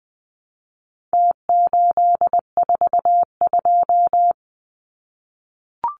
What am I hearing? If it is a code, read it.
Morse code, heard as a keyed tone, about 20 wpm, 700 Hz. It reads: T842
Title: Numbers - Contest Style - 4-Digit 20wpm